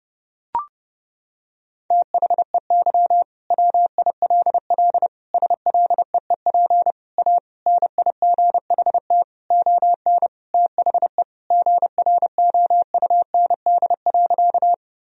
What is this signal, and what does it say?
Morse code, heard as a keyed tone, about 32 wpm, 700 Hz. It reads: THEY WILL SLEEP A NIGHT ON THE GROUND.
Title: Sentences from Top 1000 Words - Review Part 2 - 30wpm